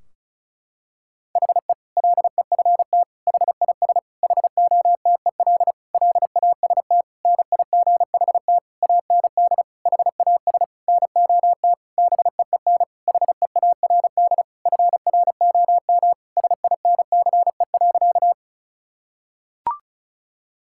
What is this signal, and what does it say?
Morse code, heard as a keyed tone, about 36 wpm, 700 Hz. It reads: HE LEFT HIS HOTEL LAST NIGHT AND HAS NOT BEEN HEARD FROM SINCE.
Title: Sentences from Top 1600 Words 35wpm